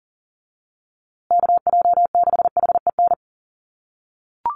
Morse code, heard as a keyed tone, about 40 wpm, 700 Hz. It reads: KJ6HEN
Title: USA Call Signs Repeated 2x 40wpm